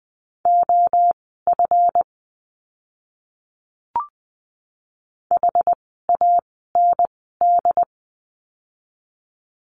Morse code, heard as a keyed tone, about 20 wpm, spaced as for 14 wpm, 700 Hz. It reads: OF HAND
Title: 244 Lesson 39f - Letter z - Words 20wpm